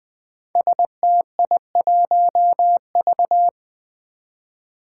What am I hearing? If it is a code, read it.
Morse code, heard as a keyed tone, about 20 wpm, 700 Hz. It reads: STI1V
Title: Five Challenging Letter-Number Combinations 20wpm